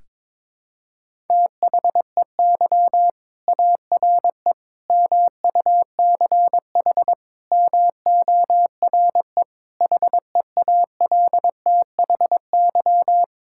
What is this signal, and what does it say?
Morse code, heard as a keyed tone, about 22 wpm, 700 Hz. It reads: THEY ARE MUCH MORE HEALTHY
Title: Sentences from Top 1000 Words - Review Part 5 - 22wpm